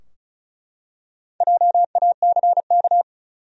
Morse code, heard as a keyed tone, about 36 wpm, 700 Hz. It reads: JACK
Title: USA Common Names - Max 4 Letters - 35wpm